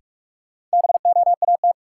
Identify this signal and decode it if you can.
Morse code, heard as a keyed tone, about 46 wpm, 700 Hz. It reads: BOAT
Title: Sentences from Top 500 Words - New Word Drill - 45wpm